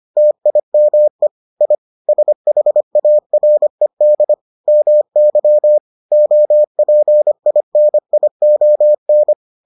Morse code, heard as a keyed tone, about 25 wpm, 600 Hz. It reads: TIME I SHARED MY OPINION